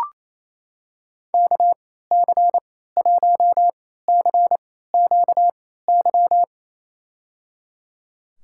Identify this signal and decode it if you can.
Morse code, heard as a keyed tone, about 28 wpm, spaced as for 16 wpm, 700 Hz. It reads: KC1CQY
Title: USA Call Signs - 6 Characters - 3x Character Spacing - 28wpm